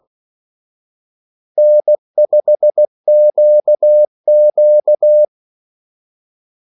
Morse code, heard as a keyed tone, about 16 wpm, 600 Hz. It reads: N5QQ
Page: Kaldesignaler 16 wpm